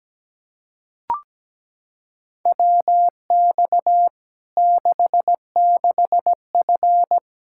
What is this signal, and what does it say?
Morse code, heard as a keyed tone, about 17 wpm, 700 Hz. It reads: WX 66F
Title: Practice QSO - Exchange 2 - Info - 17wpm